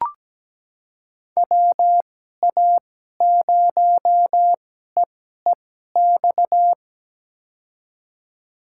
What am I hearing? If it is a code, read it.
Morse code, heard as a keyed tone, about 17 wpm, spaced as for 12 wpm, 700 Hz. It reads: WA0EEX
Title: USA Call Signs - 6 Characters - 2x Character Spacing - 17wpm